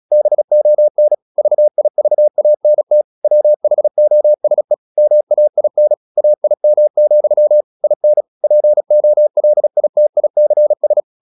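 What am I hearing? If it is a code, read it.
Morse code, heard as a keyed tone, about 36 wpm, 600 Hz. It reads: BON VIVANT WHOSE MAIN AIM, IN POLITICS
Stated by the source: Morse practice at 36 wpm, text of Ken Follett's novel Never, from CW med Gnister.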